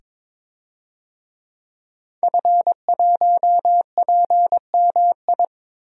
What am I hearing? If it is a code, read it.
Morse code, heard as a keyed tone, about 22 wpm, 700 Hz. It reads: F1PMI
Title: DXCC-100 Call Signs - 5 Characters - 22wpm